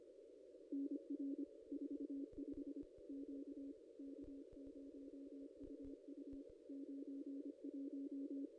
Morse code, heard as a keyed tone, about 25 wpm, 280 Hz. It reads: NR 45 Q K0UU 91
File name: contests / ss-cw-k5tr-op-n5rz -1700